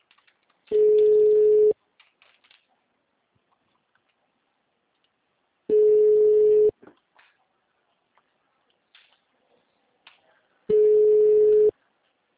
Ringback tone (Europe, 425 Hz).
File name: out- -2025- - - .12